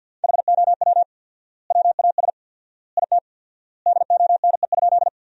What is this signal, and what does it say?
Morse code, heard as a keyed tone, about 52 wpm, 700 Hz. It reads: HOW WAS IT DONE?
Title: 4x Word-Spacing - Sentences from Top 500 Words - 50wpm